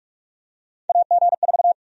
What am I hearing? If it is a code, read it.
Morse code, heard as a keyed tone, about 46 wpm, 700 Hz. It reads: AG4
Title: USA Call Sign Prefixes - 45wpm